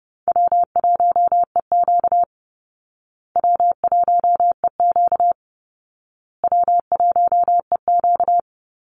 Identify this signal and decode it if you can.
Morse code, heard as a keyed tone, about 30 wpm, 700 Hz. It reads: W1EQ W1EQ W1EQ
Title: Easy Call Signs Repeated 3x - 30wpm